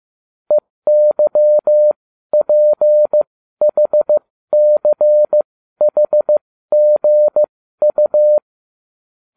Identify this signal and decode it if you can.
Morse code, heard as a keyed tone, about 15 wpm, spaced as for 12 wpm, 600 Hz. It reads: EYPHCHGU